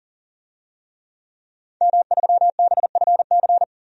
Morse code, heard as a keyed tone, about 40 wpm, 700 Hz. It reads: M3BFC
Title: DXCC-100 Call Signs - 5 Characters - 40wpm